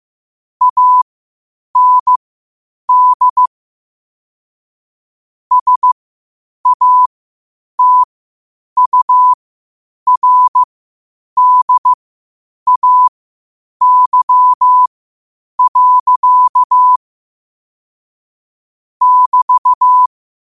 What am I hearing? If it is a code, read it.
Morse code, heard as a keyed tone, about 15 wpm, spaced as for 8 wpm, 1000 Hz. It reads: AND SATURDAY. <BT>
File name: qotdfiles / 8wpm